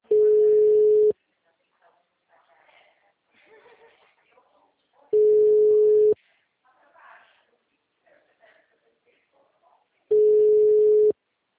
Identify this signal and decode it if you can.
Ringback tone (Europe, 425 Hz), with speech.